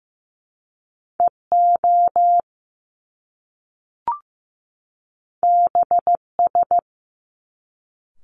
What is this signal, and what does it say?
Morse code, heard as a keyed tone, about 15 wpm, 700 Hz. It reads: EO BS
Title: Top Two Letter Combinations in Words 15wpm